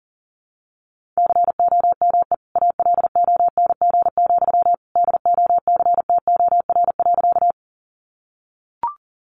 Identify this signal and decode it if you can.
Morse code, heard as a keyed tone, about 40 wpm, 700 Hz. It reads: COME ALONG, DOCTOR.
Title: Sentences from Top 1500 Words 40wpm